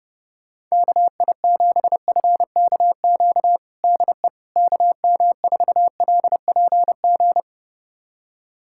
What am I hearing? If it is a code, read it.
Morse code, heard as a keyed tone, about 30 wpm, 700 Hz. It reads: KI7FKQ DE KM4LPG
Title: USA Callsign DE Callsign - 6 Characters - 30wpm